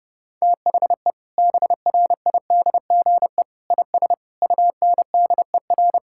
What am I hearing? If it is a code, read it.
Morse code, heard as a keyed tone, about 30 wpm, 700 Hz. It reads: THE BRIDGE IS UNDER
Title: Sentences from Top 1600 Words 30wpm